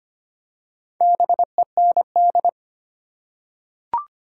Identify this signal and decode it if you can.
Morse code, heard as a keyed tone, about 25 wpm, 700 Hz. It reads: BEND
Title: Four Letter Words - 25wpm